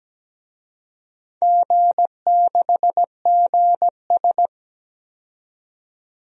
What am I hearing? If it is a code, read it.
Morse code, heard as a keyed tone, about 17 wpm, 700 Hz. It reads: G6GS
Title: DXCC-100 Call Signs - 4 Characters - 17wpm